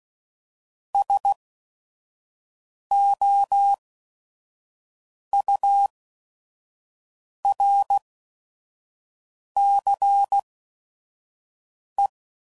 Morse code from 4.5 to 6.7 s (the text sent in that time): U